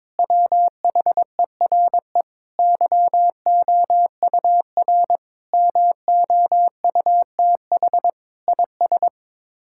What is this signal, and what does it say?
Morse code, heard as a keyed tone, about 22 wpm, 700 Hz. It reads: WHERE YOUR MOUTH IS
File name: practice / 22wpm / Sets of 7 Words - 22wpm